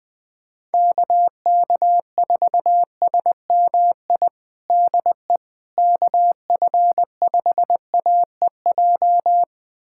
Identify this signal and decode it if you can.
Morse code, heard as a keyed tone, about 21 wpm, 700 Hz. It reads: KK4SMI DE KF5AEJ
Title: USA Callsign DE Callsign - 6 Characters - 20wpm